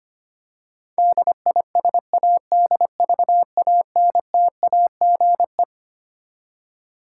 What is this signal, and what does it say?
Morse code, heard as a keyed tone, about 25 wpm, 700 Hz. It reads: DISADVANTAGE
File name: Twelve Letter Words - 25wpm